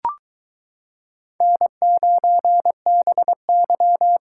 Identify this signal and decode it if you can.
Morse code, heard as a keyed tone, about 22 wpm, 700 Hz. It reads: N9BY